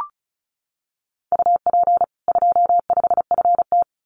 Morse code, heard as a keyed tone, about 34 wpm, 700 Hz. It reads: UP 25FT
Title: Common QSO Phrases 35wpm